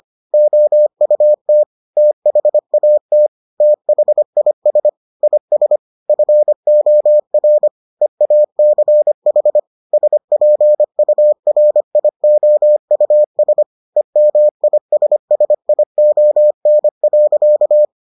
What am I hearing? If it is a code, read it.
Morse code, heard as a keyed tone, about 25 wpm, 600 Hz. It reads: OUT THAT THIS IS FOR EACH SPURIOUS EMISSION.